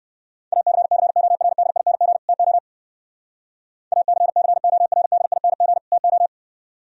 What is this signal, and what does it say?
Morse code, heard as a keyed tone, about 54 wpm, 700 Hz. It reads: ACCORDING TO ACCORDING TO
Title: Speed-Racing - Sets of 2 Words 45wpm